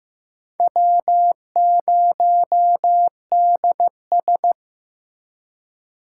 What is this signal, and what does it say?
Morse code, heard as a keyed tone, about 15 wpm, 700 Hz. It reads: W0DS
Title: USA Call Signs - 2x Character Spacing - 15wpm